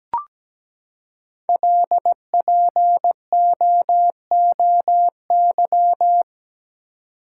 Morse code, heard as a keyed tone, about 17 wpm, 700 Hz. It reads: LPOOY
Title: Code Groups - 5 Letters - 17wpm